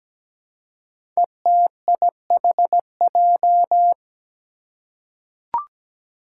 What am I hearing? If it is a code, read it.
Morse code, heard as a keyed tone, about 17 wpm, 700 Hz. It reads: ETIHJ
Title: Five Challenging Letter-Number Combinations 17wpm